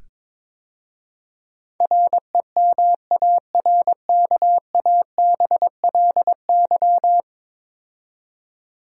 Morse code, heard as a keyed tone, about 22 wpm, 700 Hz. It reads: REMARKABLY
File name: practice / 22wpm / 10 Letter Words - First 3 Letters Given - 22wpm